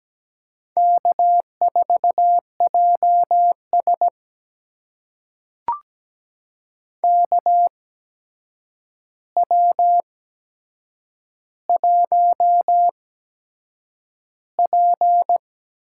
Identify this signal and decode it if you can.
Morse code, heard as a keyed tone, about 17 wpm, spaced as for 7 wpm, 700 Hz. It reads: K4JS KW1P
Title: USA Call Signs - 4 Characters - 8x Character Spacing - 17wpm